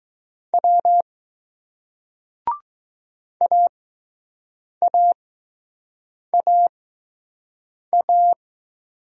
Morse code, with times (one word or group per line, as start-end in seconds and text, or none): 0.5-1.0 W
3.4-8.3 AAAA